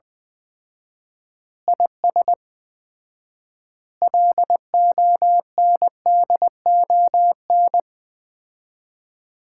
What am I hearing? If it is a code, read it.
Morse code, heard as a keyed tone, about 20 wpm, 700 Hz. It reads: IS LONDON
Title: Common QSO Phrases - 4x Word-Spacing - 20wpm